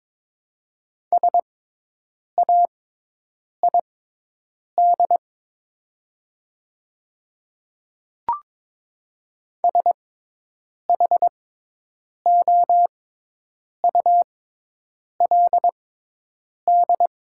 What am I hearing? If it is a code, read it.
Morse code, heard as a keyed tone, about 22 wpm, spaced as for 8 wpm, 700 Hz. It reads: SAID SHOULD
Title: 6x Character Spacing - Top 300 Words - 22wpm